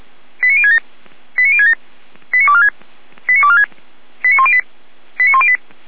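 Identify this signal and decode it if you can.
Five-tone selective call (ZVEI), heard as quick runs of five tones, tones of about 70 ms. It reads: 89067 89067 89246 89246 89108 89108
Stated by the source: ZVEI five-tone selective call 89560 received on SDR.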